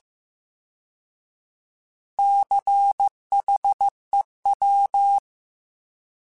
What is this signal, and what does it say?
Morse code, heard as a keyed tone, about 15 wpm, 800 Hz. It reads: CHEW